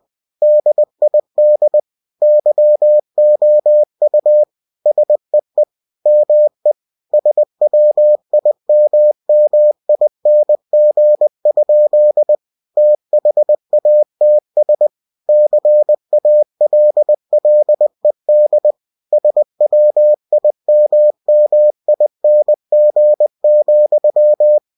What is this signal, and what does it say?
Morse code, heard as a keyed tone, about 20 wpm, 600 Hz. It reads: DID YOU SEE ME SWIMMING? THATS CALLED SWIMMING,
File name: cw-books / winnie-the-pooh / winnie-20wpm-page-0087